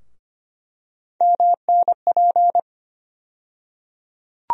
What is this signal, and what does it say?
Morse code, heard as a keyed tone, about 25 wpm, 700 Hz. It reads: MNP
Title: USA Call Sign Postfixes - 3 Letters 25wpm